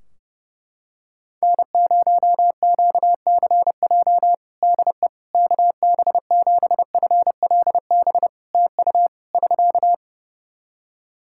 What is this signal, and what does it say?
Morse code, heard as a keyed tone, about 30 wpm, 700 Hz. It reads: N0QCJ DE KB7FLB TU <SK>
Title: Practice QSO - Exchange 4 - Ending - 30wpm